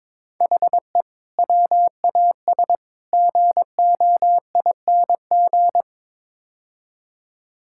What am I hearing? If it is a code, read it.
Morse code, heard as a keyed tone, about 22 wpm, 700 Hz. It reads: HE WAS GOING
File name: practice / 22wpm / Build A Phrase - 5-Word Phrases - 22wpm